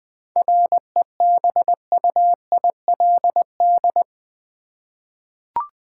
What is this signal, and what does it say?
Morse code, heard as a keyed tone, about 20 wpm, 700 Hz. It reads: REBUILD
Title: 7 Letter Words - First 3 Letters Given - 20wpm